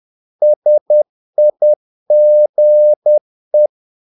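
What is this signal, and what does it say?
Morse code, heard as a keyed tone, about 10 wpm, 600 Hz. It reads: SIGE
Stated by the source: Morse practice of short Danish words at 10 wpm, from CW med Gnister.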